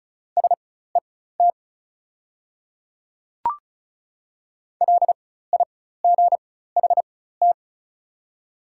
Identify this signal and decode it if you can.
Morse code, heard as a keyed tone, about 36 wpm, spaced as for 16 wpm, 700 Hz. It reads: SET LIGHT
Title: 4x Character Spacing - Top 300 Words - 35wpm